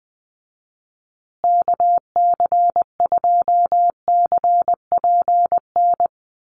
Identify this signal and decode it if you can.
Morse code, heard as a keyed tone, about 20 wpm, 700 Hz. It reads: KC2CPN
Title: USA Call Signs Repeated 3x 20wpm